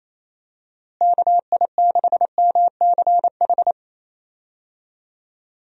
Morse code, heard as a keyed tone, about 27 wpm, 700 Hz. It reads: KI6MCH
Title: USA Call Signs - 6 Characters - Repeated 2x - 28wpm